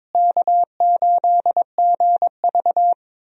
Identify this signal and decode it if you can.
Morse code, heard as a keyed tone, about 22 wpm, 700 Hz. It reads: K8GV
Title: USA Call Signs - 4 Characters - Repeated 2x - 22wpm